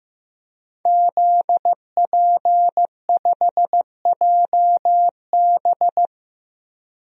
Morse code, heard as a keyed tone, about 15 wpm, 700 Hz. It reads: ZP5JB
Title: DXCC-100 Call Signs - 5 Characters - Repeated 2x - 15wpm